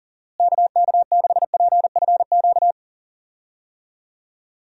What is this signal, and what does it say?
Morse code, heard as a keyed tone, about 40 wpm, 700 Hz. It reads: KK6PFQ
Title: USA Call Signs - 6 Characters - Repeated 3x - 40wpm